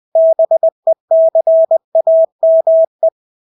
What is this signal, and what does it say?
Morse code, heard as a keyed tone, about 20 wpm, 650 Hz. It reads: BECAME
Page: Engelske ord 20 wpm